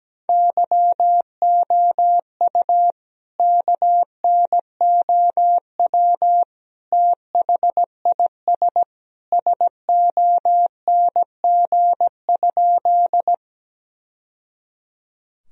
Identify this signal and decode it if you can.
Morse code, heard as a keyed tone, about 17 wpm, 700 Hz. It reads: YOU KNOW THIS SONG?
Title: Sentences from Top 1000 Words - Review Part 2 - 17wpm